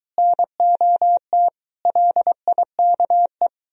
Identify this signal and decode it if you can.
Morse code, heard as a keyed tone, about 23 wpm, 700 Hz. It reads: NOT LIKE